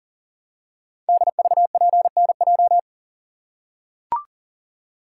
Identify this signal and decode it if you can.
Morse code, heard as a keyed tone, about 38 wpm, 700 Hz. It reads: DVPNJ
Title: Code Groups - 5 Letters - 40wpm